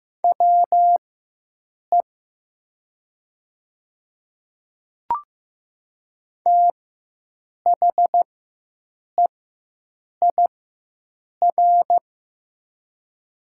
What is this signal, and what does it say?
Morse code, heard as a keyed tone, about 15 wpm, spaced as for 7 wpm, 700 Hz. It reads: WE THEIR
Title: 4x Character Spacing - Top 200 Words - 15wpm